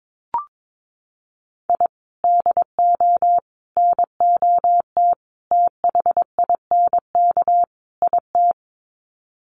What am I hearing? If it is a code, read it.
Morse code, heard as a keyed tone, about 22 wpm, 700 Hz. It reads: I DO NOT THINK IT
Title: Sets of 5 Words 22wpm